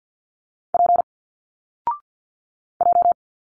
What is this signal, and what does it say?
Morse code, heard as a keyed tone, about 46 wpm, 700 Hz. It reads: P J
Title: Single Letters 50wpm